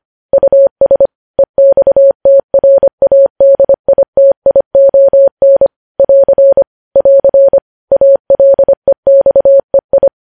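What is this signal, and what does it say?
Morse code, heard as a keyed tone, about 25 wpm, 550 Hz. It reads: US EXTRADITION <AR> <AR> ALEXEI
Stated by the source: Morse code practice of news headlines sent at 25 wpm.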